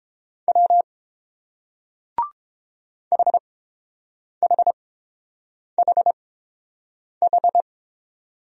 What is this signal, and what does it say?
Morse code, heard as a keyed tone, about 29 wpm, spaced as for 7 wpm, 700 Hz. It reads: W HHHH